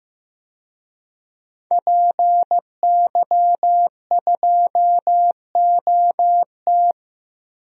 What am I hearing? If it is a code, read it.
Morse code, heard as a keyed tone, about 15 wpm, 700 Hz. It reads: PY2OT
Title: DXCC-100 Call Signs - 5 Characters - Repeated 3x - 15wpm